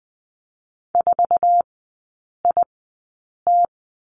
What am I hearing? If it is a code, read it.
Morse code, heard as a keyed tone, about 20 wpm, spaced as for 8 wpm, 700 Hz. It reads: 4IT
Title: 107 Lesson 18e - Letter m - Three Characters 20wpm